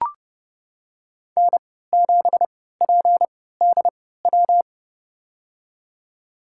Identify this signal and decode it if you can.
Morse code, heard as a keyed tone, about 31 wpm, spaced as for 17 wpm, 700 Hz. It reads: N7PDW
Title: USA Call Signs - 3x Character Spacing - 30wpm